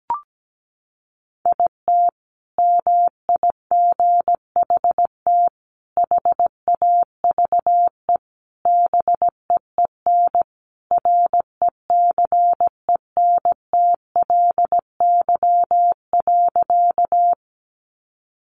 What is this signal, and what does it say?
Morse code, heard as a keyed tone, about 17 wpm, 700 Hz. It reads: IT MIGHT HAVE BEEN RECENTLY.
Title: Sentences from Top 1200 Words 17wpm